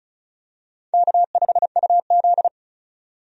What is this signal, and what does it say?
Morse code, heard as a keyed tone, about 36 wpm, 700 Hz. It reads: K5UZ
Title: USA Call Signs - 4 Characters - 35wpm